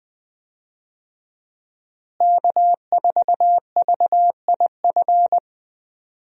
Morse code, heard as a keyed tone, about 21 wpm, 700 Hz. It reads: K4VIF
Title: USA Call Signs - 5 Characters - Repeated 2x - 20wpm